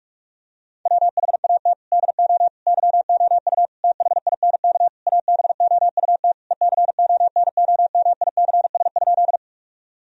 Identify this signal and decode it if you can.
Morse code, heard as a keyed tone, about 46 wpm, 700 Hz. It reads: WHAT DO YOU THINK ABOUT ECONOMICS?
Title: Sentences from Top 1000 Words - Review Part 3 - 45wpm